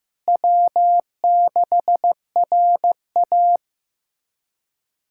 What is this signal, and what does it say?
Morse code, heard as a keyed tone, about 16 wpm, 700 Hz. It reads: W6RA
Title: K1USN SST QSO Practice - 15wpm